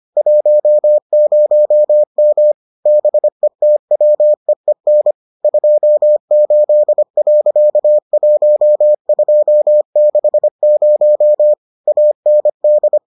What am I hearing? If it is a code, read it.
Morse code, heard as a keyed tone, about 25 wpm, 600 Hz. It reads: 10M BETWEEN 28.1260 AND